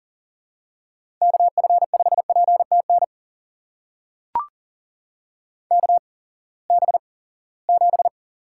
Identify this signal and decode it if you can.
Morse code, heard as a keyed tone, about 40 wpm, spaced as for 23 wpm, 700 Hz. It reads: KF5PTN KB7
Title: USA Call Signs - 6 Characters - 8x Character Spacing - 40wpm